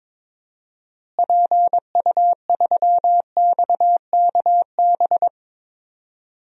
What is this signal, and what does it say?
Morse code, heard as a keyed tone, about 21 wpm, 700 Hz. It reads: PU3XKB
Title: DXCC-100 Call Signs - 6 Characters - Repeated 2x - 22wpm